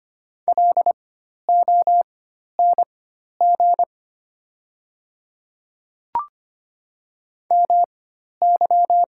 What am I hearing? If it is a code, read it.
Morse code, heard as a keyed tone, about 25 wpm, spaced as for 12 wpm, 700 Hz. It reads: LONG MY
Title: 4x Character Spacing - Top 200 Words - 25wpm